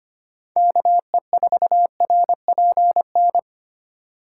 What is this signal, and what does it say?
Morse code, heard as a keyed tone, about 25 wpm, 700 Hz. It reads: KE4RPN
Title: USA Call Signs - 6 Characters - 2x Character Spacing - 25wpm